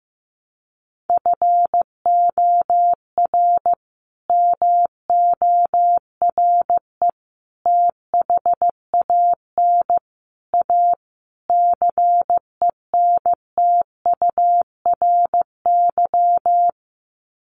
Morse code, heard as a keyed tone, about 15 wpm, 700 Hz. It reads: FOR MORE THAN A CENTURY
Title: Sets of 5 Words Encore2 15wpm